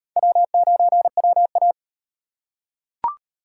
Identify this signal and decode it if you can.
Morse code, heard as a keyed tone, about 35 wpm, 700 Hz. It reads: W9WA